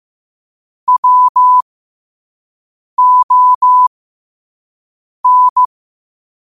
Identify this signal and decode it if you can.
Morse code, heard as a keyed tone, about 15 wpm, spaced as for 5 wpm, 1000 Hz. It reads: WON
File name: 5wpm